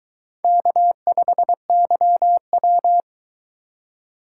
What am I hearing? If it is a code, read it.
Morse code, heard as a keyed tone, about 22 wpm, 700 Hz. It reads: K5YW